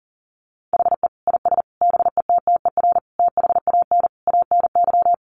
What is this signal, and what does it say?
Morse code, heard as a keyed tone, about 40 wpm, 700 Hz. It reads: HE IS BETTER THAN ANY
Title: Sentences from Top 400 Words 40wpm